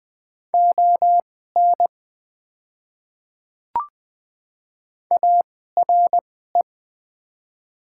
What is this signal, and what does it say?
Morse code, heard as a keyed tone, about 20 wpm, spaced as for 14 wpm, 700 Hz. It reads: ON ARE
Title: 2x Character Spacing - Top 100 Words - 20wpm